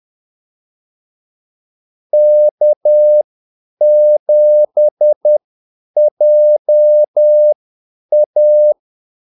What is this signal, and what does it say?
Morse code, heard as a keyed tone, about 10 wpm, spaced as for 8 wpm, 600 Hz. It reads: K7JA
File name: Callsigns-10-8wpm